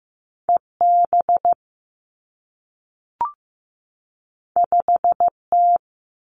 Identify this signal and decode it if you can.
Morse code, heard as a keyed tone, about 15 wpm, 700 Hz. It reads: EB 5T
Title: Two Challenging Letter-Number Combinations 15wpm